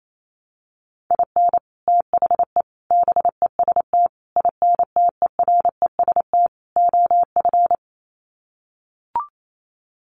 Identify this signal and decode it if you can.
Morse code, heard as a keyed tone, about 28 wpm, 700 Hz. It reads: IN THE BEST INTEREST OF
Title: Sets of 5 Words Encore 28wpm